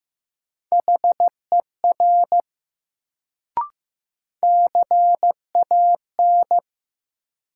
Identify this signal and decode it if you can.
Morse code, heard as a keyed tone, about 15 wpm, 700 Hz. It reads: HER CAN
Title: Sentences from Top 100 Words - New Word Drill - 15wpm